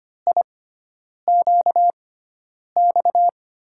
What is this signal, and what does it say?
Morse code, heard as a keyed tone, about 25 wpm, spaced as for 9 wpm, 700 Hz. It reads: IQX
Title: USA Call Signs - 6x Character Spacing - 25wpm